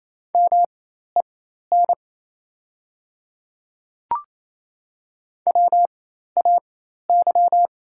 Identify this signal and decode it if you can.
Morse code, heard as a keyed tone, about 28 wpm, spaced as for 13 wpm, 700 Hz. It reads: MEN WAY
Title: 4x Character Spacing - Top 300 Words - 28wpm